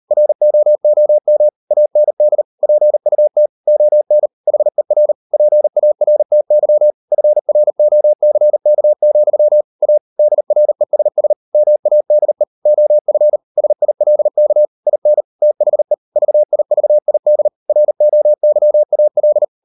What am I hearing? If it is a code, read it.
Morse code, heard as a keyed tone, about 40 wpm, 600 Hz. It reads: ROOM AND PUT ON HER PARTY FROCK, A DRESS MADE OF SILK IN THE VIVID ROYAL